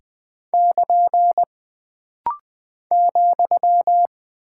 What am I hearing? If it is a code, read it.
Morse code, heard as a keyed tone, about 20 wpm, 700 Hz. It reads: <KN> ,
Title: Single Punctuation-Prosign - Rapid-Fire - 20wpm